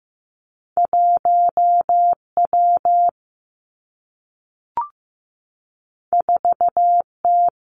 Morse code, heard as a keyed tone, about 15 wpm, 700 Hz. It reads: 1 W 4 T
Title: Two Challenging Letter-Number Combinations 15wpm